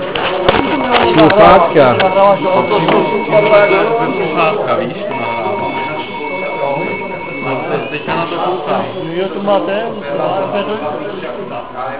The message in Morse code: none